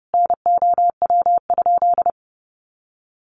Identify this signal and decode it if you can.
Morse code, heard as a keyed tone, about 30 wpm, 700 Hz. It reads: NOW?
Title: Sentences from Top 200 Words 30wpm